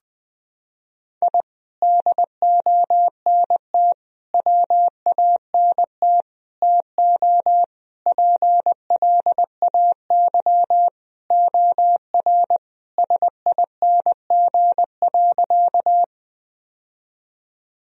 Morse code, heard as a keyed tone, about 21 wpm, 700 Hz. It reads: I DONT WANT TO PLAY OR SING.
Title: Sentences from Top 1000 Words - Review Part 2 - 20wpm